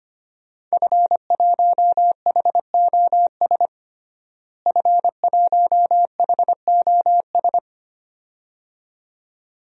Morse code, heard as a keyed tone, about 25 wpm, 700 Hz. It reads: F1HOS F1HOS
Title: DXCC-100 Call Signs - 5 Characters - Repeated 2x - 25wpm